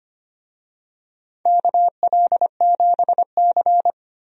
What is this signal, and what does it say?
Morse code, heard as a keyed tone, about 25 wpm, 700 Hz. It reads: KL7C
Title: DXCC-100 Call Signs - 4 Characters - 25wpm